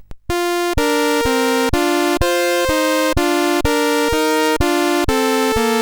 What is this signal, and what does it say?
There is no code.